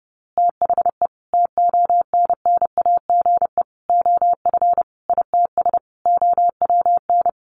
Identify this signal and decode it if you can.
Morse code, heard as a keyed tone, about 30 wpm, 700 Hz. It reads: THE TONNAGE OF ITS OWN
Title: A Sound Of Thunder 30wpm